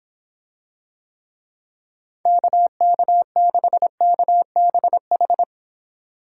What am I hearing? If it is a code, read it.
Morse code, heard as a keyed tone, about 25 wpm, 700 Hz. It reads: KK6KBH